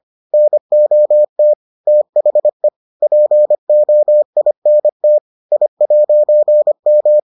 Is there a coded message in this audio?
Yes: Morse code from 0.3 to 7.2 s, heard as a keyed tone, about 25 wpm, 600 Hz. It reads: NOT THE POINT I'M